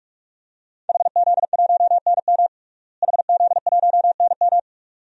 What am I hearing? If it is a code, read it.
Morse code, heard as a keyed tone, about 46 wpm, 700 Hz. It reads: HZ1NM HZ1NM
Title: DXCC-100 Call Signs - 5 Characters - Repeated 2x - 45wpm